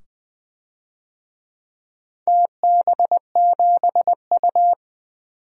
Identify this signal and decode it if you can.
Morse code, heard as a keyed tone, about 20 wpm, 700 Hz. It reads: TB7U
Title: DXCC-100 Call Signs - 4 Characters - 20wpm